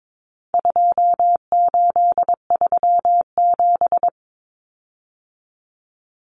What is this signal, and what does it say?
Morse code, heard as a keyed tone, about 22 wpm, 700 Hz. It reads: 2837
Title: Common Cut Numbers - 4-Digit 22wpm